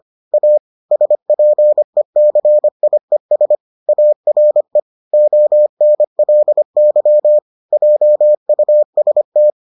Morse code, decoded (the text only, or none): A SPECIES ARE ONLY JUST